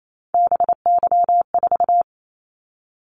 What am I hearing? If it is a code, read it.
Morse code, heard as a keyed tone, about 28 wpm, 700 Hz. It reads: BY4
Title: Callsign Prefixes for Top 100 DXCC - 28wpm